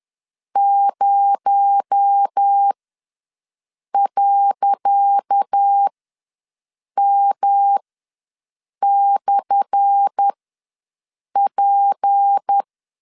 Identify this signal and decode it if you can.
Morse code, heard as a keyed tone, about 10 wpm, spaced as for 6 wpm, 800 Hz. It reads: 0.M/P
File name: LEZIONI / 10 SIGNS / 10C DOT ws 10wpm 4213